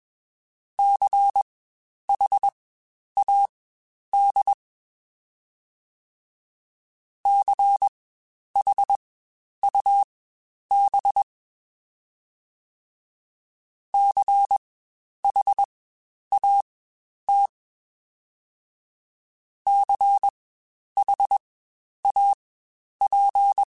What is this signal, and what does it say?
Morse code, heard as a keyed tone, about 21 wpm, spaced as for 10 wpm, 800 Hz. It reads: CHAD CHUB CHAT CHAP
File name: Spaced Alphas / Spaced Alphas mp3s / Cs sp @20